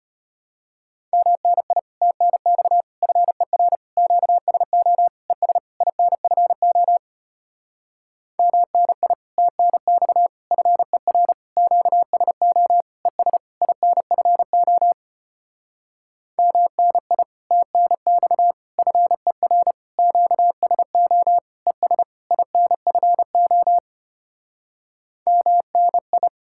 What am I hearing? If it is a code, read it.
Morse code, heard as a keyed tone, about 33 wpm, 700 Hz. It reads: MNI TNX FER QSO ES INFO MNI TNX FER QSO ES INFO MNI TNX FER QSO ES INFO MNI